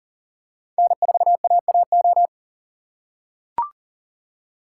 Morse code, heard as a keyed tone, about 40 wpm, 700 Hz. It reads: N4AAO
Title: USA Call Signs - 5 Characters - 2x Character Spacing - 40wpm